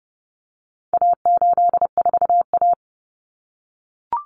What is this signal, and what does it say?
Morse code, heard as a keyed tone, about 30 wpm, 700 Hz. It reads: A84A
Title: Common Cut Numbers - 4-Digit 30wpm